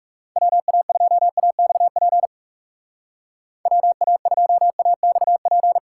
Morse code, heard as a keyed tone, about 37 wpm, 700 Hz. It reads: WA2AXP WA2AXP
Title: Speed Racing - USA Call Signs 30wpm